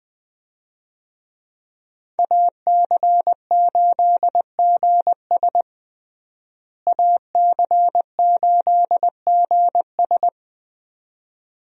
Morse code, heard as a keyed tone, about 21 wpm, 700 Hz. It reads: AC8GS AC8GS
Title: USA Call Signs - 5 Characters - Repeated 2x - 20wpm